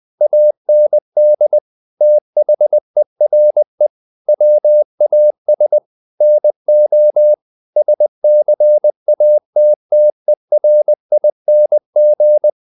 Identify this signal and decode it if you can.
Morse code, heard as a keyed tone, about 20 wpm, 600 Hz. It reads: AND THERE WAS NO SCATTERING
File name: the old man and the sea / the old man and the sea part 1 20wpm